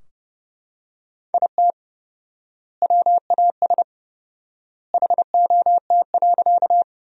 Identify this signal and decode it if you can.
Morse code, heard as a keyed tone, about 30 wpm, 700 Hz. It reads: IT WAS HOT.
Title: 4x Word-Spacing - Sentences from Top 500 Words - 30wpm